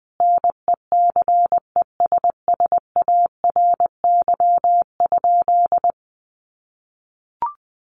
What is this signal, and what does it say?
Morse code, heard as a keyed tone, about 20 wpm, 700 Hz. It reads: NECESSARY?
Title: Sentences from Top 800 Words 20wpm